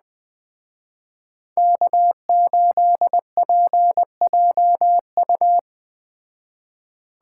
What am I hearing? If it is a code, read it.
Morse code, heard as a keyed tone, about 20 wpm, 700 Hz. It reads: K8PJU
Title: Speed Racing - USA 1x3 Call Signs 17wpm